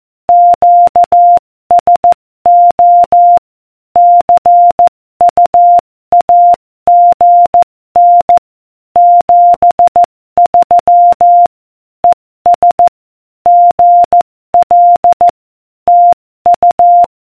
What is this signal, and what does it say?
Morse code, heard as a keyed tone, about 14 wpm, 700 Hz. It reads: QSO CUAGN 73 ES GL TU